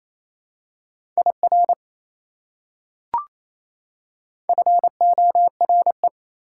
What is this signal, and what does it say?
Morse code, heard as a keyed tone, about 28 wpm, 700 Hz. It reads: IR FORE
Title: Prefixes and Suffixes 28wpm